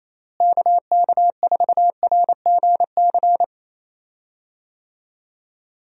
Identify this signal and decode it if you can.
Morse code, heard as a keyed tone, about 28 wpm, 700 Hz. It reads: KK4RGC
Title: USA Call Signs - 6 Characters - Repeated 3x - 28wpm